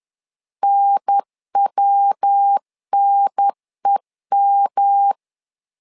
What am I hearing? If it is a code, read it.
Morse code, heard as a keyed tone, about 10 wpm, 800 Hz. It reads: NWNEM
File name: LEZIONI / 3 MWRO / 3 MW 10wpm 1959